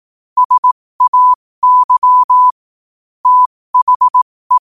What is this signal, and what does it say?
Morse code, heard as a keyed tone, about 18 wpm, spaced as for 15 wpm, 1000 Hz. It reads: SAY THE
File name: qotdfiles / 15wpm